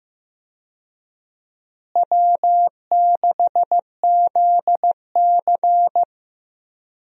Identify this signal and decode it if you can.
Morse code, heard as a keyed tone, about 15 wpm, 700 Hz. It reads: W6ZC
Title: USA Call Signs - 4 Characters - Repeated 2x - 15wpm